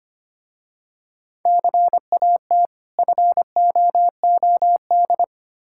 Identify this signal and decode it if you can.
Morse code, heard as a keyed tone, about 25 wpm, 700 Hz. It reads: CAT FOOD
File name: practice / 25wpm / Difficult Words with Hints - 25wpm